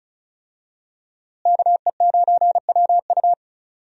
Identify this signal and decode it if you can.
Morse code, heard as a keyed tone, about 36 wpm, 700 Hz. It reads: KE9WU
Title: USA Call Signs - 5 Characters - Repeated 2x - 35wpm